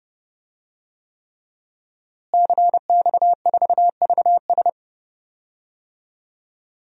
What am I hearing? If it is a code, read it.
Morse code, heard as a keyed tone, about 31 wpm, 700 Hz. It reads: CX4VS
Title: DXCC-100 Call Signs - 5 Characters - Repeated 3x - 30wpm